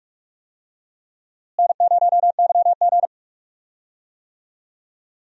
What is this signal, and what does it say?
Morse code, heard as a keyed tone, about 47 wpm, 700 Hz. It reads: N0YG
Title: USA Call Signs - 4 Characters - Repeated 2x - 45wpm